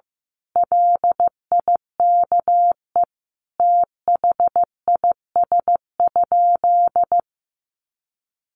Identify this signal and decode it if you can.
Morse code, heard as a keyed tone, about 15 wpm, 700 Hz. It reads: LIKE THIS?
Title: Sentences from Top 1400 Words 15wpm